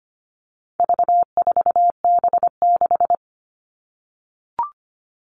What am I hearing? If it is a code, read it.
Morse code, heard as a keyed tone, about 25 wpm, 700 Hz. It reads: V4B6
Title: All Cut Numbers - 4-Digit 25wpm